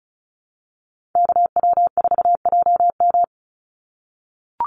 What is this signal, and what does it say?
Morse code, heard as a keyed tone, about 34 wpm, 700 Hz. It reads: KW4JM
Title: NA SOTA Chaser Callsigns 35wpm